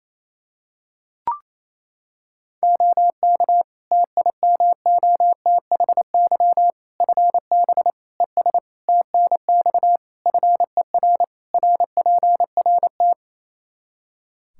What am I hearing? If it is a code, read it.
Morse code, heard as a keyed tone, about 28 wpm, 700 Hz. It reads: OK TIMOTHY FB ES TNX FER RPRT
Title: K1USN SST QSO Practice 28wpm